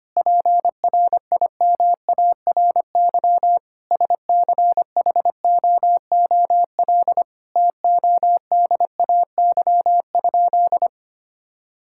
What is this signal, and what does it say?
Morse code, heard as a keyed tone, about 25 wpm, 700 Hz. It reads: PRIMARY SCHOOL TODAY?
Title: Sentences from Top 1000 Words - Review Part 5 - 25wpm